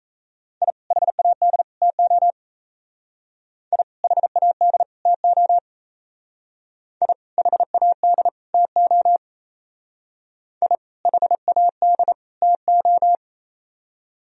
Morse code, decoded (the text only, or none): I HAD TO I HAD TO I HAD TO I HAD TO